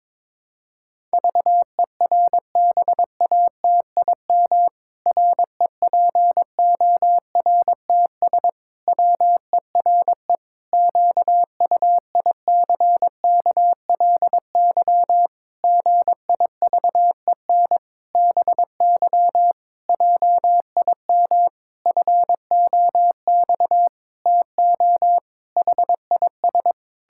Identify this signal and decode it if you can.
Morse code, heard as a keyed tone, about 22 wpm, 700 Hz. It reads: VERBATIM REPORTS WERE QUICKLY GIVEN BY JIM FOX TO HIS
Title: Pangram Challenge v2 - 22wpm